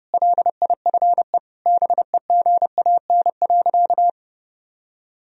Morse code, heard as a keyed tone, about 30 wpm, 700 Hz. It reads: LIFE BEGAN.
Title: Sentences from Top 1000 Words - Review Part 1 - 30wpm